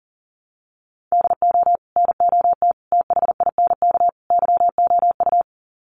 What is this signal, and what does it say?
Morse code, heard as a keyed tone, about 37 wpm, 700 Hz. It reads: DO NOT THINK YOU
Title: Sets of 4 Words 40wpm